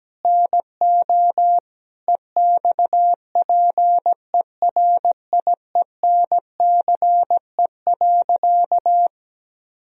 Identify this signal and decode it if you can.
Morse code, heard as a keyed tone, about 17 wpm, 700 Hz. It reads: NO EXPERIENCE.
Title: Sentences from Top 1000 Words - Review Part 3 - 17wpm